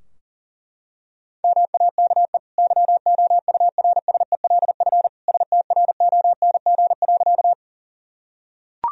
Morse code, heard as a keyed tone, about 40 wpm, 700 Hz. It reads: MAKE YOURSELF STRONG.
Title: Sentences from Top 1000 Words - Review Part 3 - 40wpm